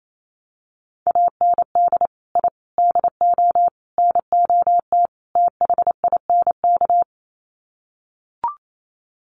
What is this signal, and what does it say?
Morse code, heard as a keyed tone, about 28 wpm, 700 Hz. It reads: AND I DO NOT THINK
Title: Sets of 5 Words 28wpm